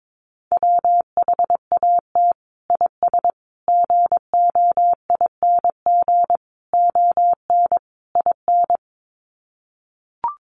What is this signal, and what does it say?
Morse code, heard as a keyed tone, about 22 wpm, 700 Hz. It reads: WHAT IS GOING ON IN
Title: Sets of 5 Words Encore2 22wpm